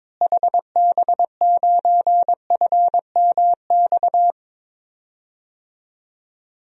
Morse code, heard as a keyed tone, about 22 wpm, 700 Hz. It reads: HB9FMX
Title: DXCC-100 Call Signs - 6 Characters - Repeated 2x - 22wpm